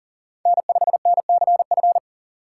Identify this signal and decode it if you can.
Morse code, heard as a keyed tone, about 40 wpm, 700 Hz. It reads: N5NCF
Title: USA Call Signs - 5 Characters - 6x Character Spacing - 40wpm